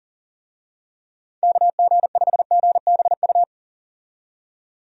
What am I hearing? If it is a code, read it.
Morse code, heard as a keyed tone, about 38 wpm, 700 Hz. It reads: KG5GBU
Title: USA Call Signs - 6 Characters - 40wpm